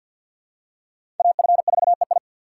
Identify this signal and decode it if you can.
Morse code, heard as a keyed tone, about 50 wpm, 700 Hz. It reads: AF4EI
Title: USA Call Signs - 5 Characters - 3x Character Spacing - 50wpm